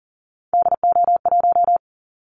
Morse code, heard as a keyed tone, about 40 wpm, 700 Hz. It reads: DO1
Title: Callsign Prefixes for Top 25 DXCC - 40wpm